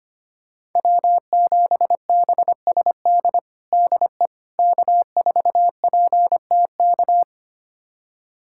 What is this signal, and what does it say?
Morse code, heard as a keyed tone, about 25 wpm, 700 Hz. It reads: W7BSD DE K4PTK
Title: USA Callsign DE Callsign - 5 Characters - 25wpm